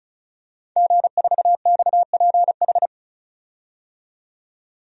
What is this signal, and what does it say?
Morse code, heard as a keyed tone, about 36 wpm, 700 Hz. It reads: G4XPH
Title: DXCC-100 Call Signs - 5 Characters - 35wpm